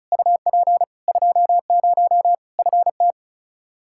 Morse code, heard as a keyed tone, about 34 wpm, 700 Hz. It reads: UP 20 FT
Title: Practice QSO - Exchange 2 - Info - 35wpm